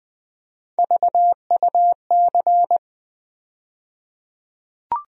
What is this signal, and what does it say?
Morse code, heard as a keyed tone, about 20 wpm, 700 Hz. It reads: VUC
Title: USA Call Sign Postfixes - 3 Letters 20wpm